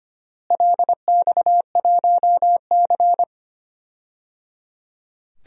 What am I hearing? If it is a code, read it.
Morse code, heard as a keyed tone, about 25 wpm, 700 Hz. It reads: LX1C
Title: DXCC-100 Call Signs - 4 Characters - Repeated 3x - 25wpm